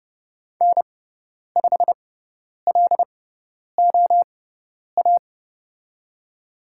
Morse code, heard as a keyed tone, about 29 wpm, spaced as for 10 wpm, 700 Hz. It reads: N5LOA
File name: Lesson 17g - Letter u - Call Signs 30wpm